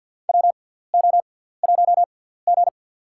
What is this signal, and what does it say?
Morse code, heard as a keyed tone, about 51 wpm, spaced as for 17 wpm, 700 Hz. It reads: WO1G
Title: USA Call Signs - 6x Character Spacing - 50wpm